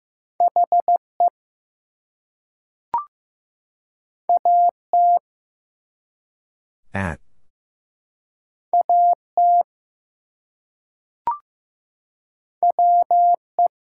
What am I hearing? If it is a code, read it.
Morse code, heard as a keyed tone, about 15 wpm, 700 Hz. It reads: HE AT AT WE
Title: Two Letter Words - 15wpm